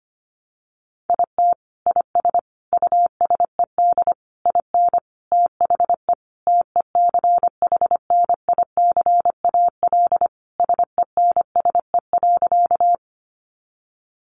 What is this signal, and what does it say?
Morse code, heard as a keyed tone, about 25 wpm, 700 Hz. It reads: IT IS USED IN THE TECHNICAL SENSE.
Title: Sentences from Top 1100 Words 25wpm